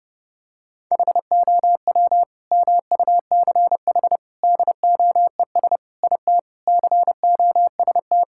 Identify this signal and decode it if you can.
Morse code, heard as a keyed tone, about 30 wpm, 700 Hz. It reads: HOW MUCH DOES IT COST
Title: Sentences from Top 1600 Words 30wpm